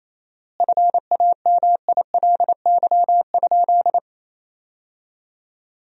Morse code, heard as a keyed tone, about 28 wpm, 700 Hz. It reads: FAMILY?
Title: 4x Word-Spacing - Sentences from Top 400 Words - 28wpm